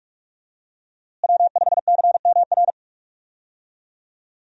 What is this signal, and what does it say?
Morse code, heard as a keyed tone, about 42 wpm, 700 Hz. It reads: W5CMR
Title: USA Call Signs - 5 Characters - Repeated 3x - 45wpm